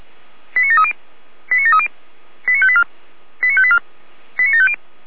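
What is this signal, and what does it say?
Five-tone selective call (ZVEI), heard as quick runs of five tones, tones of about 70 ms. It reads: 89730 89730 89584 89584 89750
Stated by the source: ZVEI five-tone call received on SDR.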